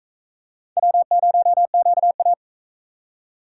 Morse code, heard as a keyed tone, about 43 wpm, 700 Hz. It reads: W0QA